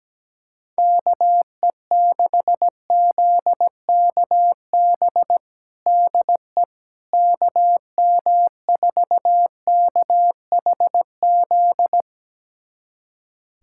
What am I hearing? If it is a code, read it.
Morse code, heard as a keyed tone, about 17 wpm, 700 Hz. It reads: KE6ZKB DE KM4KHZ
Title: USA Callsign DE Callsign 17wpm